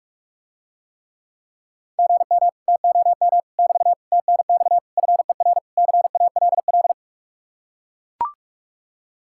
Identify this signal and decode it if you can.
Morse code, heard as a keyed tone, about 44 wpm, 700 Hz. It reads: GM TOM <BT> TNX FER CALL